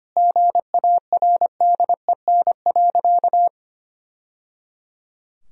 Morse code, heard as a keyed tone, about 25 wpm, 700 Hz. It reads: GARDEN.
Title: Sentences from Top 1000 Words - Review Part 4 - 25wpm